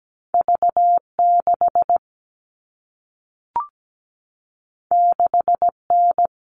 Morse code, heard as a keyed tone, about 17 wpm, 700 Hz. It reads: V6 6N
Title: All Cut Numbers - 2-Digit 17wpm